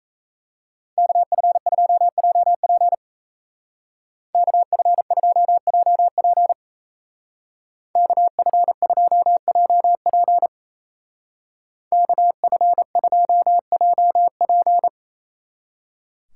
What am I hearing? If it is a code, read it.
Morse code, heard as a keyed tone, about 34 wpm, 700 Hz. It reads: KF2JP KF2JP KF2JP KF2JP
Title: Speed Racing - USA 2x2 Call Signs 28wpm